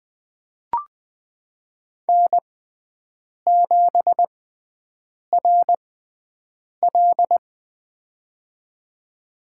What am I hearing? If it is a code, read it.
Morse code, heard as a keyed tone, about 20 wpm, spaced as for 7 wpm, 700 Hz. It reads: N7RL
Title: USA Call Signs - 6x Character Spacing - 20wpm